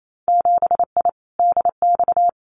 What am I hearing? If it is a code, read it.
Morse code, heard as a keyed tone, about 28 wpm, 700 Hz. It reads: 7I DX
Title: Winter Field Day Exchanges Encore 28wpm